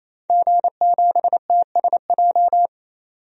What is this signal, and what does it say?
Morse code, heard as a keyed tone, about 27 wpm, 700 Hz. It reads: G7TSJ
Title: DXCC-100 Call Signs - 5 Characters - 28wpm